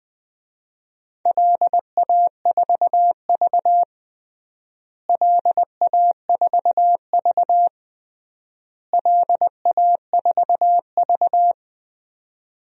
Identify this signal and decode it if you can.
Morse code, heard as a keyed tone, about 20 wpm, 700 Hz. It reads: LA4V LA4V LA4V
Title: DXCC-100 Call Signs - 4 Characters - Repeated 3x - 20wpm